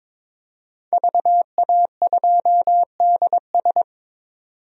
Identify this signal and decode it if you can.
Morse code, heard as a keyed tone, about 22 wpm, 700 Hz. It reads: VA2DS
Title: DXCC-100 Call Signs - 5 Characters - 22wpm